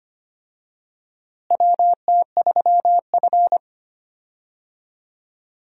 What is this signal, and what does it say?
Morse code, heard as a keyed tone, about 25 wpm, 700 Hz. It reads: WT3F
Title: USA Call Signs - 4 Characters - Repeated 3x - 25wpm